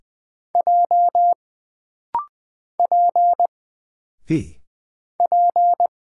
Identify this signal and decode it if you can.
Morse code, heard as a keyed tone, about 20 wpm, 700 Hz. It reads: J P P
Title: Single Character (Letter:Number:Punctuation:Prosign) - Rapid-Fire - 20wpm